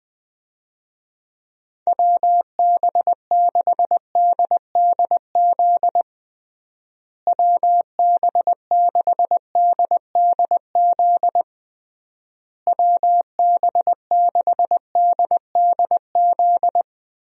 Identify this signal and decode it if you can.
Morse code, heard as a keyed tone, about 20 wpm, 700 Hz. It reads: WB6DDZ WB6DDZ WB6DDZ
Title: USA Call Signs - 6 Characters - Repeated 3x - 20wpm